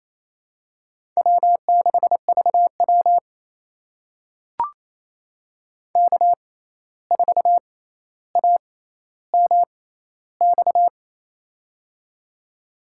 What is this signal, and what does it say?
Morse code, heard as a keyed tone, about 28 wpm, spaced as for 13 wpm, 700 Hz. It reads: W6VW K4AMX
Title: USA Call Signs - 6x Character Spacing - 28wpm